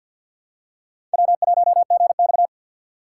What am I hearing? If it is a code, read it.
Morse code, heard as a keyed tone, about 52 wpm, 700 Hz. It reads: W1GX
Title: USA Call Signs - 4 Characters - 2x Character Spacing - 50wpm